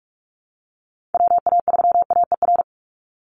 Morse code, heard as a keyed tone, about 46 wpm, 700 Hz. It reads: WA3AER
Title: Call Signs - 45wpm